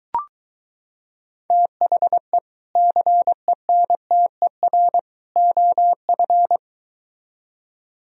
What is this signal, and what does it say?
Morse code, heard as a keyed tone, about 23 wpm, 700 Hz. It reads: THE CENTER OF